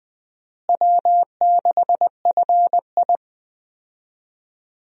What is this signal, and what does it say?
Morse code, heard as a keyed tone, about 20 wpm, 700 Hz. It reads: W6FI
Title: USA Call Signs - 4 Characters - Repeated 3x - 20wpm